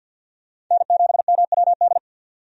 Morse code, heard as a keyed tone, about 46 wpm, 700 Hz. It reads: N7MWD
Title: USA Call Signs - 5 Characters - 50wpm